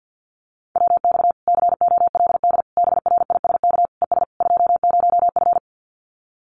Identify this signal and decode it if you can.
Morse code, heard as a keyed tone, about 50 wpm, 700 Hz. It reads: WX COLD BRISK ES 20F
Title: Common QSO Phrases Encore 50wpm